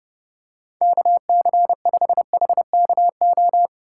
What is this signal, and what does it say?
Morse code, heard as a keyed tone, about 30 wpm, 700 Hz. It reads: KC5HKO
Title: USA Call Signs - 6 Characters - 3x Character Spacing - 30wpm